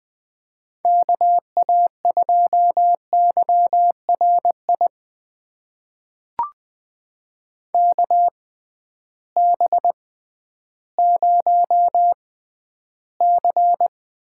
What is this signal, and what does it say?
Morse code, heard as a keyed tone, about 20 wpm, spaced as for 12 wpm, 700 Hz. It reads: KA2YRI KB0C
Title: USA Call Signs - 6 Characters - 6x Character Spacing - 20wpm